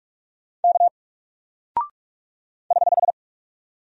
Morse code, heard as a keyed tone, about 44 wpm, 700 Hz. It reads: K <HH>
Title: Single Character (Letter:Number:Punctuation:Prosign) Alt 45wpm